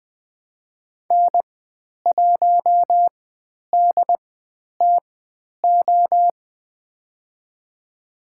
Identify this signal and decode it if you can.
Morse code, heard as a keyed tone, about 20 wpm, spaced as for 10 wpm, 700 Hz. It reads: N1DTO
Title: 095 Lesson 16g - Letter c - Call Signs 20wpm